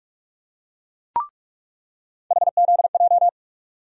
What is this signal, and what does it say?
Morse code, heard as a keyed tone, about 42 wpm, 700 Hz. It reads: HZJ
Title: USA Call Sign Postfixes - 3 Letters 45wpm